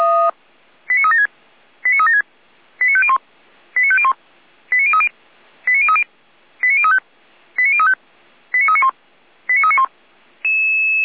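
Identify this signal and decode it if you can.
Five-tone selective call (ZVEI), heard as quick runs of five tones, tones of about 80 ms. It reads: 89376 89376 89501 89501 89030 89030 89035 89035 89391 89391
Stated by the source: ZVEI five-tone selective call received on an SDR.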